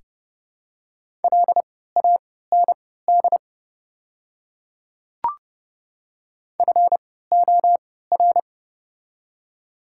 Morse code, heard as a keyed tone, about 31 wpm, spaced as for 17 wpm, 700 Hz. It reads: LAND FOR
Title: 3x Character Spacing - Top 500 Words - 30wpm